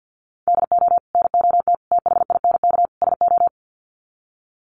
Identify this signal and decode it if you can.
Morse code, heard as a keyed tone, about 50 wpm, 700 Hz. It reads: DO NOT THINK SO
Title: Sets of 4 Words 50wpm